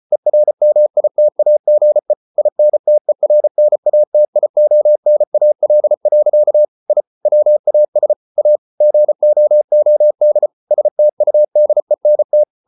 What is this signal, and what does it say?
Morse code, heard as a keyed tone, about 34 wpm, 600 Hz. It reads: ERMITAGE INTERNATIONAL. I WAS A GOOD STUDENT